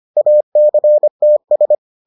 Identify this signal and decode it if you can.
Morse code, heard as a keyed tone, about 25 wpm, 600 Hz. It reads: ACTS